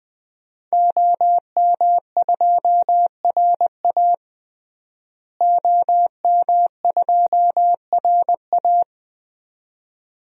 Morse code, heard as a keyed tone, about 20 wpm, 700 Hz. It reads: OM2RA OM2RA
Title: DXCC-100 Call Signs - 5 Characters - Repeated 2x - 20wpm